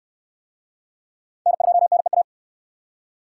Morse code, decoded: N2NEA

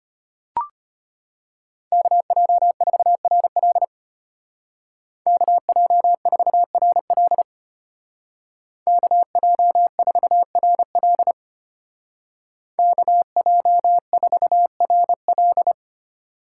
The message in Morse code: KJ4RL KJ4RL KJ4RL KJ4RL

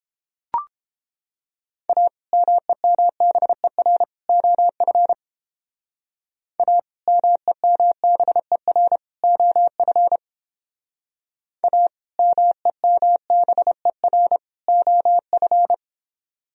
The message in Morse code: A MEMBER OF A MEMBER OF A MEMBER OF

